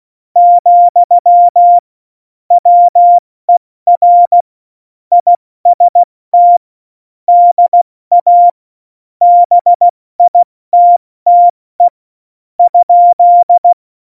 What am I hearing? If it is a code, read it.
Morse code, heard as a keyed tone, about 16 wpm, spaced as for 13 wpm, 700 Hz. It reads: , WER IST DA BITTE?
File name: Morsekurs / Witze / Witz 05. Juni 2018, 70 F80